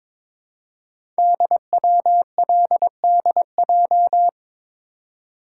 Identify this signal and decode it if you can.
Morse code, heard as a keyed tone, about 22 wpm, 700 Hz. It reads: DWLDJ
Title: Code Groups - 5 Letters - 22wpm